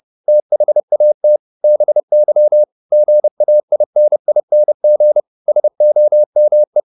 Morse code, heard as a keyed tone, about 30 wpm, 600 Hz. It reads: THAT BY GAINING SOME